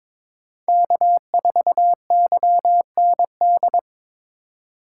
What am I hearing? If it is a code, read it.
Morse code, heard as a keyed tone, about 21 wpm, 700 Hz. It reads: K4YND
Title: K1USN SST QSO Practice - 22wpm